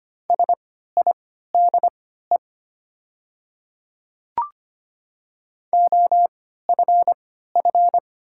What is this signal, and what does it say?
Morse code, heard as a keyed tone, about 25 wpm, spaced as for 14 wpm, 700 Hz. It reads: SIDE OFF
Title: 3x Character Spacing - Top 500 Words - 25wpm